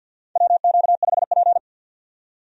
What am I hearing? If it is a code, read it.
Morse code, heard as a keyed tone, about 53 wpm, 700 Hz. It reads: WQ5P
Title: USA Call Signs - 4 Characters - 50wpm